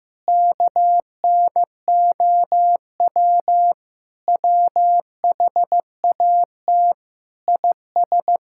Morse code, heard as a keyed tone, about 15 wpm, 700 Hz. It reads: KNOW WHAT IS
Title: Sentences from Top 1000 Words - Review Part 5 - 15wpm